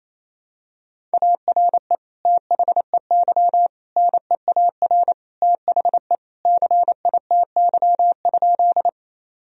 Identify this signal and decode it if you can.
Morse code, heard as a keyed tone, about 27 wpm, 700 Hz. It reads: ARE THEY NEAR THE CITY?
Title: Sentences from Top 1000 Words - Review Part 1 - 28wpm